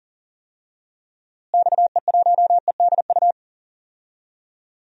Morse code, heard as a keyed tone, about 40 wpm, 700 Hz. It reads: XE1EDU
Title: DXCC-100 Call Signs - 6 Characters - Repeated 3x - 40wpm